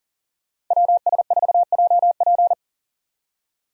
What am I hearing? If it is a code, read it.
Morse code, heard as a keyed tone, about 40 wpm, 700 Hz. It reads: WS4JP
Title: USA Call Signs - 5 Characters - 40wpm